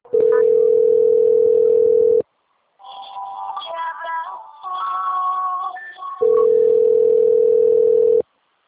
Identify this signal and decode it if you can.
Ringback tone (North America), 440 + 480 Hz.
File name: out- - - - - .45536